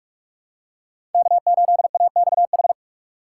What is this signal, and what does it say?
Morse code, heard as a keyed tone, about 46 wpm, 700 Hz. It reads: K8AXH
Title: USA Call Signs - 5 Characters - 3x Character Spacing - 45wpm